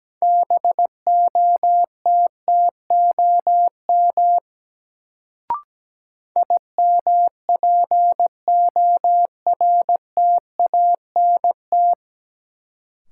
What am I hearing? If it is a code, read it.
Morse code, heard as a keyed tone, about 17 wpm, 700 Hz. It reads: BOTTOM IMPORTANT
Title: Sets of 3 Words - New Word Drill - 17wpm